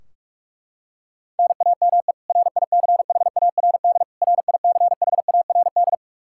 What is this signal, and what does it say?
Morse code, heard as a keyed tone, about 41 wpm, 700 Hz. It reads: NAME RICHARD RICHARD
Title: Practice QSO - Exchange 1 - Beginning - 45wpm